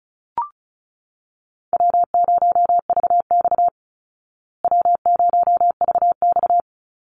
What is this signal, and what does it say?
Morse code, heard as a keyed tone, about 34 wpm, 700 Hz. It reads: W0VX W0VX
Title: Easy Call Signs Repeated 2x - 35wpm